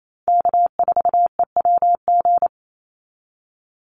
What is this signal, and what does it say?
Morse code, heard as a keyed tone, about 28 wpm, 700 Hz. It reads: K4EWG
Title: Call Signs Repeated 3x - 28wpm